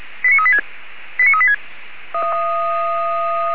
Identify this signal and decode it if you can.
Five-tone selective call (ZVEI), heard as quick runs of five tones, tones of about 70 ms. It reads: 89386 89386
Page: /464/SDR/ZVEI/89138/